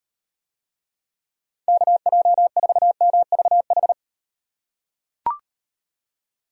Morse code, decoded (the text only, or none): KJ4MVH